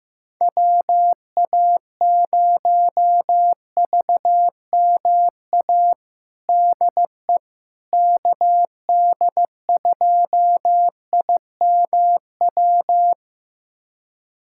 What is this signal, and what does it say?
Morse code, heard as a keyed tone, about 15 wpm, 700 Hz. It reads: WA0VMA DE KD2IMW
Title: USA Callsign DE Callsign 15wpm